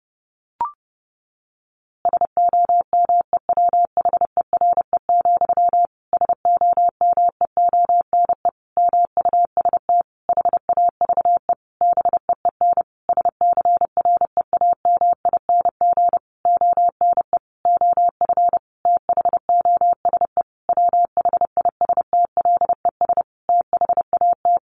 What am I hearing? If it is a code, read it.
Morse code, heard as a keyed tone, about 30 wpm, 700 Hz. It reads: SOMEWHERE, SOMEONE MUST HAVE BEEN SCREAMING ONE OF THOSE WHISTLES THAT
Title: A Sound Of Thunder 30wpm